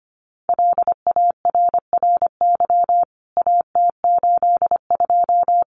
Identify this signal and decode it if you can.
Morse code, heard as a keyed tone, about 25 wpm, 700 Hz. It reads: LARRY AT82
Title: CWT Exchange 25wpm